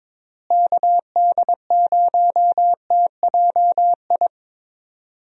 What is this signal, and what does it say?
Morse code, heard as a keyed tone, about 21 wpm, 700 Hz. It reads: KD0TJI
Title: USA Call Signs - 6 Characters - Repeated 3x - 22wpm